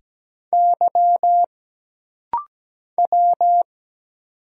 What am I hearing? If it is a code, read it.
Morse code, heard as a keyed tone, about 17 wpm, 700 Hz. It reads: Y W
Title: Single Letters - Rapid-Fire 17wpm